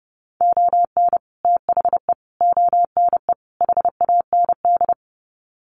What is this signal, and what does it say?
Morse code, heard as a keyed tone, about 30 wpm, 700 Hz. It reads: ON THE ONE HAND
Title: Sets of 4 Words 30wpm